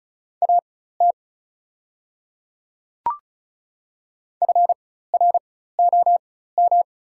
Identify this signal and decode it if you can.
Morse code, heard as a keyed tone, about 36 wpm, spaced as for 16 wpm, 690 Hz. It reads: AT FROM
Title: 4x Character Spacing - Top 200 Words - 35wpm